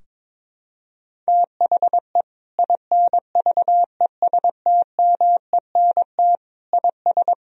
Morse code, heard as a keyed tone, about 22 wpm, 700 Hz. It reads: THE INVESTMENT IS
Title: Sentences from Top 1000 Words - Review Part 4 - 22wpm